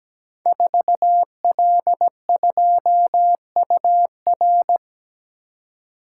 Morse code, heard as a keyed tone, about 17 wpm, 700 Hz. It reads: 4L2UR
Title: Code Groups - 5 Letter:Numbers - 17wpm